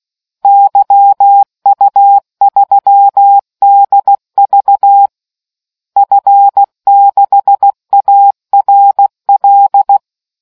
Morse code, heard as a keyed tone, about 16 wpm, 800 Hz. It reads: YU3DV F6ARL